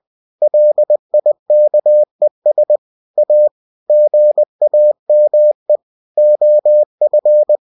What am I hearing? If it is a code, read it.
Morse code, heard as a keyed tone, about 20 wpm, 600 Hz. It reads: LIKES A GAME OF